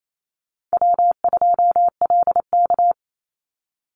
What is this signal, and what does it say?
Morse code, heard as a keyed tone, about 28 wpm, 700 Hz. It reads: W2LK
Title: Easy Call Signs - 28wpm